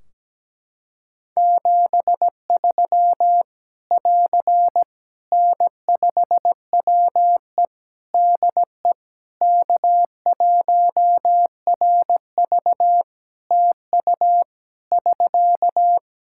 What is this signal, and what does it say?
Morse code, heard as a keyed tone, about 17 wpm, 700 Hz. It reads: 73 <AR> N5WE DE K1RV TU <SK>
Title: Practice QSO - Exchange 4 - Ending - 17wpm